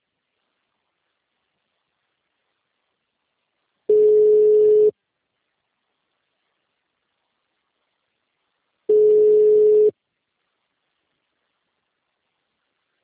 Ringback tone (Europe, 425 Hz).